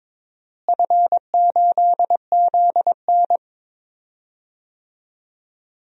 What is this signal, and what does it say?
Morse code, heard as a keyed tone, about 22 wpm, 700 Hz. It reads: F8ZN
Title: DXCC-100 Call Signs - 4 Characters - Repeated 3x - 22wpm